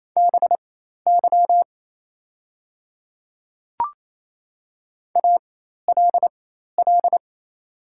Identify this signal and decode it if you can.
Morse code, heard as a keyed tone, about 28 wpm, spaced as for 13 wpm, 700 Hz. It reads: BY ALL
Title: 4x Character Spacing - Top 100 Words - 28wpm